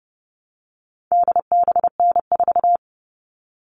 Morse code, heard as a keyed tone, about 30 wpm, 700 Hz. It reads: DBN4
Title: All Cut Numbers - 4-Digit 30wpm